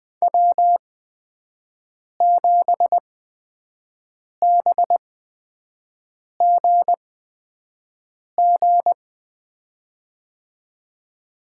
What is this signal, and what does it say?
Morse code, heard as a keyed tone, about 20 wpm, spaced as for 5 wpm, 700 Hz. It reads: W7BGG
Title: USA Call Signs - 8x Character Spacing - 20wpm